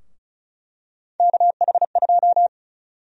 Morse code, heard as a keyed tone, about 36 wpm, 700 Hz. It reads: KH2
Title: USA Call Sign Prefixes - Repeated 2x - 35wpm